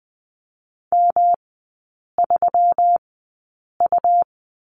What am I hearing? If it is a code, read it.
Morse code, heard as a keyed tone, about 20 wpm, spaced as for 8 wpm, 700 Hz. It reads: M3U
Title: 138 Lesson 23e - Letter f - Three Characters 20wpm